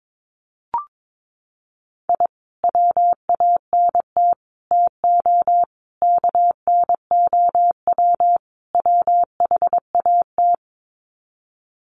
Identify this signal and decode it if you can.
Morse code, heard as a keyed tone, about 22 wpm, 700 Hz. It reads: I WANT TO KNOW WHAT
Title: Sets of 5 Words Encore2 22wpm